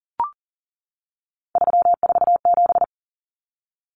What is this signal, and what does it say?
Morse code, heard as a keyed tone, about 40 wpm, 700 Hz. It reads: 347
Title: Numbers - 3-Digit 40wpm